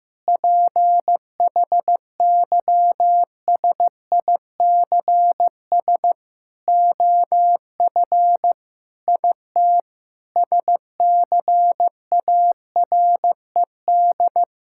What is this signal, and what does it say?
Morse code, heard as a keyed tone, about 15 wpm, 700 Hz. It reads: PHYSICS OF IT SCARED
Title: Sentences from Top 1000 Words - Review Part 5 - 15wpm